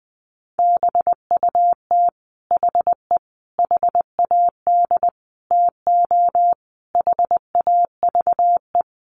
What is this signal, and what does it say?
Morse code, heard as a keyed tone, about 20 wpm, 700 Hz. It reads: BUT HE HAD TO HAVE
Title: Sentences from Top 100 Words 20wpm